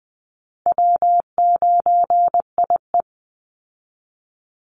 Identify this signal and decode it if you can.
Morse code, heard as a keyed tone, about 20 wpm, 700 Hz. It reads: W9IE
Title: Call Signs Repeated 2x - 20wpm